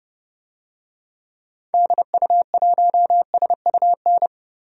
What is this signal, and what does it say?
Morse code, heard as a keyed tone, about 30 wpm, 700 Hz. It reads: DU1SUN
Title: DXCC-100 Call Signs - 6 Characters - 30wpm